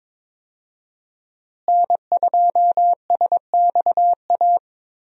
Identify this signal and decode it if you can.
Morse code, heard as a keyed tone, about 21 wpm, 700 Hz. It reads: N2SXA
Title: USA Call Signs - 5 Characters - Repeated 2x - 22wpm